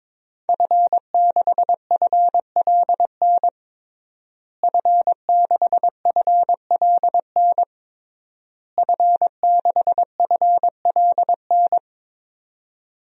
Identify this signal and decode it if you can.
Morse code, heard as a keyed tone, about 22 wpm, 700 Hz. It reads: F6FLN F6FLN F6FLN
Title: DXCC-100 Call Signs - 5 Characters - Repeated 3x - 22wpm